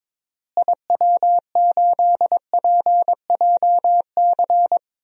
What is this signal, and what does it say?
Morse code, heard as a keyed tone, about 22 wpm, 700 Hz. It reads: IW8PJC
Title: DXCC-100 Call Signs - 6 Characters - Repeated 2x - 22wpm